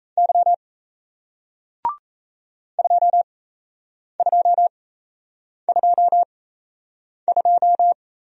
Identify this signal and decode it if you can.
Morse code, heard as a keyed tone, about 38 wpm, spaced as for 8 wpm, 700 Hz. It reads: Y 2222